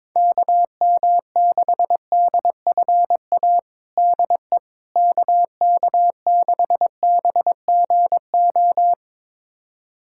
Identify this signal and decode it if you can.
Morse code, heard as a keyed tone, about 22 wpm, 700 Hz. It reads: KM6DFA DE KK6BGO
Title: USA Callsign DE Callsign - 6 Characters - 22wpm